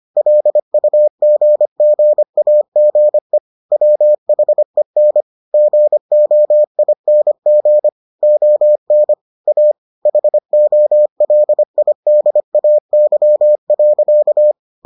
Morse code, heard as a keyed tone, about 25 wpm, 600 Hz. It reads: LUGGAGE WHEN GOING ON A HOLIDAY.